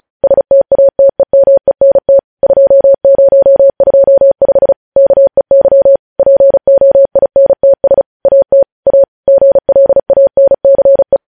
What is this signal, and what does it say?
Morse code, heard as a keyed tone, about 36 wpm, 550 Hz. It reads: STATEMENT 2025 KEY POINTS AT A GLANCE
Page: News Headlines in Morse Code at 35 WPM